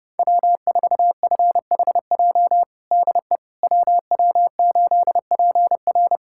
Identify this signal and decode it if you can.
Morse code, heard as a keyed tone, about 32 wpm, 700 Hz. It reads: W4FHJ DE WW8PR
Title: USA Callsign DE Callsign - 5 Characters - 30wpm